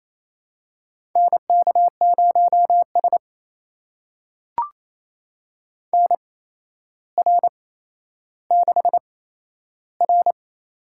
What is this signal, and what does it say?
Morse code, heard as a keyed tone, about 28 wpm, spaced as for 12 wpm, 700 Hz. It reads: NK0S NR6R
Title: USA Call Signs - 4 Characters - 8x Character Spacing - 28wpm